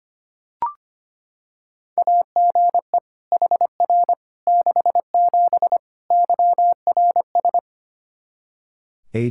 Morse code, heard as a keyed tone, about 25 wpm, 700 Hz. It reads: AGE HR 67 YRS
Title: Practice QSO - Exchange 3 - More Info - 25wpm